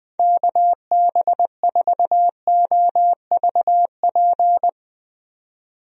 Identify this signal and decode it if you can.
Morse code, heard as a keyed tone, about 20 wpm, 700 Hz. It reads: KB4OVP
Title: USA Call Signs - 6 Characters - 20wpm